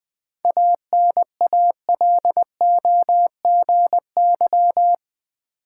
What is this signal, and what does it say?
Morse code, heard as a keyed tone, about 20 wpm, 700 Hz. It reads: ANALOGY
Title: 7 Letter Words - First 3 Letters Given - 20wpm